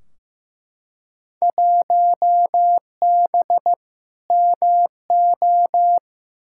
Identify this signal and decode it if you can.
Morse code, heard as a keyed tone, about 15 wpm, 700 Hz. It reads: 1B MO
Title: Field Day Exchanges - 15wpm